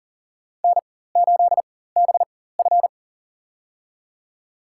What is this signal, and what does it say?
Morse code, heard as a keyed tone, about 40 wpm, spaced as for 19 wpm, 700 Hz. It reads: N8BF
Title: USA Call Signs - 4x Character Spacing - 40wpm